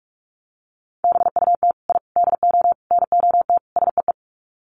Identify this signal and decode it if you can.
Morse code, heard as a keyed tone, about 44 wpm, 700 Hz. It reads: BUT I DO NOT SEE
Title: Sets of 5 Words Encore 45wpm